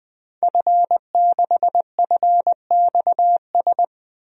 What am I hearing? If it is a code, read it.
Morse code, heard as a keyed tone, about 21 wpm, 700 Hz. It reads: F6FXS
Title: DXCC-100 Call Signs - 5 Characters - 20wpm